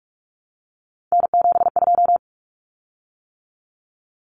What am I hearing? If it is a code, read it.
Morse code, heard as a keyed tone, about 44 wpm, 700 Hz. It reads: N72
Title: Common Cut Numbers - 3-Digit 45wpm